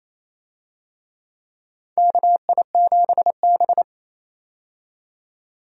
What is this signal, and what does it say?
Morse code, heard as a keyed tone, about 28 wpm, 700 Hz. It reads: KI7B
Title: USA Call Signs - 4 Characters - 28wpm